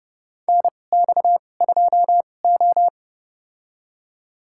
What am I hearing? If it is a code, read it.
Morse code, heard as a keyed tone, about 31 wpm, spaced as for 22 wpm, 700 Hz. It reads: NX2O
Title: USA Call Signs - 4 Characters - 2x Character Spacing - 30wpm